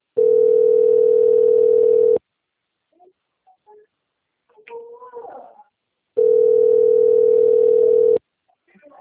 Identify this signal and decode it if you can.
Ringback tone (North America), 440 + 480 Hz.
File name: out- - - - - .38489